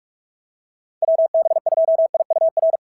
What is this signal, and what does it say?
Morse code, heard as a keyed tone, about 41 wpm, 690 Hz. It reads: WB2IUR
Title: SOTA QSO Practice 45wpm